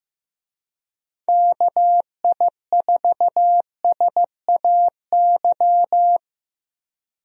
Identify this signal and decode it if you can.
Morse code, heard as a keyed tone, about 15 wpm, 700 Hz. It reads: KI4SAY
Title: USA Call Signs - 6 Characters - Repeated 2x - 15wpm